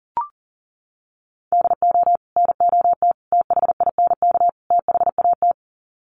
Morse code, heard as a keyed tone, about 40 wpm, 700 Hz. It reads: DO NOT THINK THAT
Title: Sets of 4 Words 40wpm